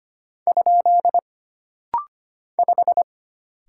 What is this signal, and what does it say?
Morse code, heard as a keyed tone, about 25 wpm, 700 Hz. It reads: ? 5
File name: Single Character (Letter:Number:Punctuation:Prosign) - Rapid-Fire - 25wpm